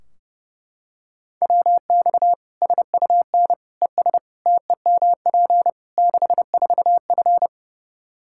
Morse code, heard as a keyed tone, about 30 wpm, 700 Hz. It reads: WX SUN ES TEMP 64F
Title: Practice QSO - Exchange 2 - Info - 30wpm